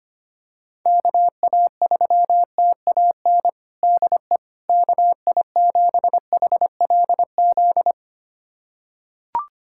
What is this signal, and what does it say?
Morse code, heard as a keyed tone, about 25 wpm, 700 Hz. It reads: KA3TAN DE KI7HLZ
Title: USA Callsign DE Callsign - 6 Characters - 25wpm